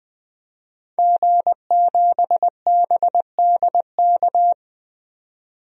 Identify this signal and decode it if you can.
Morse code, heard as a keyed tone, about 20 wpm, 700 Hz. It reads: G7BDK
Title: DXCC-100 Call Signs - 5 Characters - 20wpm